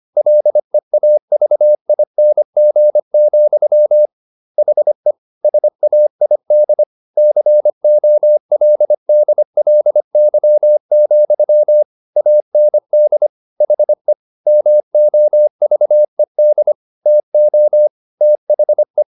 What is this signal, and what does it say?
Morse code, heard as a keyed tone, about 25 wpm, 600 Hz. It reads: LEAVING, HE SAID COLDLY, AND HE MOVED TO THE